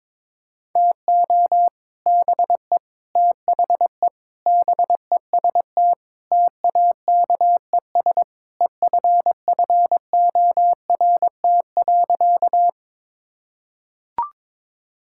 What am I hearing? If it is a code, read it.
Morse code, heard as a keyed tone, about 22 wpm, 700 Hz. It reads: TO BE THE BEST TAKES EFFORT.
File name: practice / 22wpm / Sentences with 3-Word Phrases - 22wpm